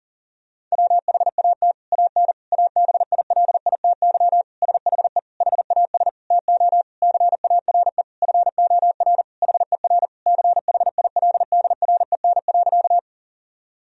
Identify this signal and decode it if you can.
Morse code, heard as a keyed tone, about 40 wpm, 700 Hz. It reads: WHAT AN ABILITY SHE HAS TO CARE FOR HER CHILDREN.
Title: Sentences from Top 1000 Words - Review Part 3 - 40wpm